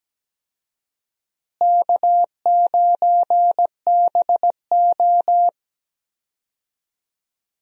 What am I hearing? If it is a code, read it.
Morse code, heard as a keyed tone, about 17 wpm, 700 Hz. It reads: K9BO
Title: USA Call Signs - 4 Characters - Repeated 2x - 17wpm